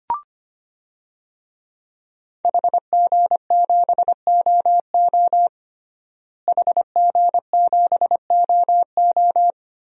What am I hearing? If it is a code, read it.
Morse code, heard as a keyed tone, about 25 wpm, 700 Hz. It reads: HG7OO HG7OO
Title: DXCC-100 Call Signs - 5 Characters - Repeated 3x - 25wpm